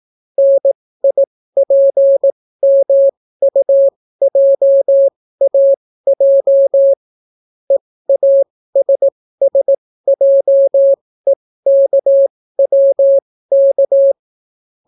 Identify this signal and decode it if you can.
Morse code, heard as a keyed tone, about 18 wpm, spaced as for 14 wpm, 550 Hz. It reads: NIPMUJAJ EASSJEKWK